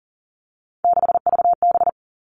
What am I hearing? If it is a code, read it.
Morse code, heard as a keyed tone, about 40 wpm, 700 Hz. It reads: 6VB
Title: All Cut Numbers - 3-Digit 40wpm